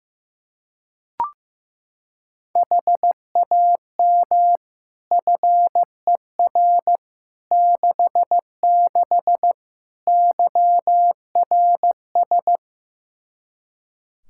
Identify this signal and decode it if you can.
Morse code, heard as a keyed tone, about 15 wpm, 700 Hz. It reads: HAM FER 66 YRS
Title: Practice QSO - Exchange 3 - More Info - 15wpm